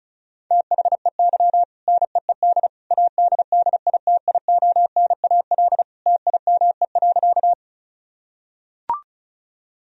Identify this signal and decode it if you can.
Morse code, heard as a keyed tone, about 36 wpm, 700 Hz. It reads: THEY NEED ADDITIONAL TIME.
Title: Sentences from Top 1000 Words - Review Part 4 - 35wpm